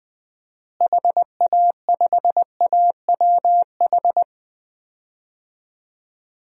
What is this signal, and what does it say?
Morse code, heard as a keyed tone, about 20 wpm, 700 Hz. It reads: HA5AWH
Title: DXCC-100 Call Signs - 6 Characters - Repeated 3x - 20wpm